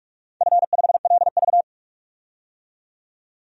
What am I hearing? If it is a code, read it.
Morse code, heard as a keyed tone, about 47 wpm, 700 Hz. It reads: F5LV
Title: DXCC-100 Call Signs - 4 Characters - 45wpm